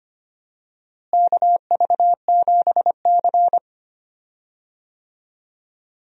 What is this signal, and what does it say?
Morse code, heard as a keyed tone, about 25 wpm, 700 Hz. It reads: KV7C
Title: USA Call Signs - 4 Characters - Repeated 3x - 25wpm